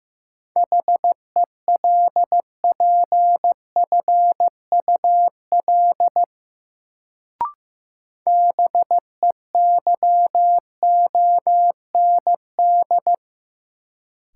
Morse code, heard as a keyed tone, about 15 wpm, 700 Hz. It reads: HELPFUL BEYOND
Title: Sentences from Top 1000 Words - Review Part 6 - New Word Drill - 15wpm